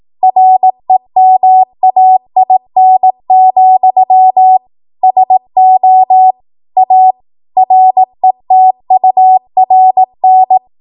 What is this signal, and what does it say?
Morse code, heard as a keyed tone, about 18 wpm, 750 Hz. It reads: REMAIN, SO A RETURN
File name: Morse / 18WPM